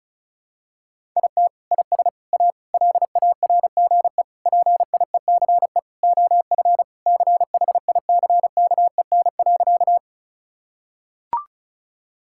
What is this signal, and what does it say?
Morse code, heard as a keyed tone, about 34 wpm, 700 Hz. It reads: IT IS A LARGE PIECE OF CHICKEN.
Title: Sentences from Top 1000 Words - Review Part 5 - 35wpm